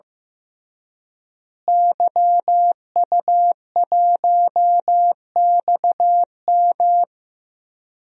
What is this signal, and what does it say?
Morse code, heard as a keyed tone, about 15 wpm, 700 Hz. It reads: YU1XM
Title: DXCC-100 Call Signs - 5 Characters - Repeated 3x - 15wpm